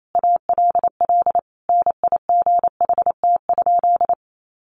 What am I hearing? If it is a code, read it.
Morse code, heard as a keyed tone, about 28 wpm, 700 Hz. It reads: ALL NIGHT?
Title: Sentences from Top 700 Words 28wpm